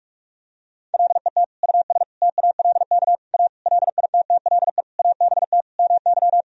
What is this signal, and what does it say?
Morse code, heard as a keyed tone, about 46 wpm, 700 Hz. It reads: LET US TALK A LITTLE ABT MY
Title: Sentences from Top 400 Words and Abbreviations 45wpm